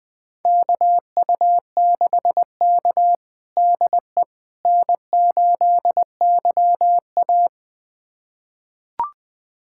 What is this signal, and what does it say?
Morse code, heard as a keyed tone, about 21 wpm, 700 Hz. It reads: KU6K DE N8YA
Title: USA Callsign DE Callsign - 4 Characters - 20wpm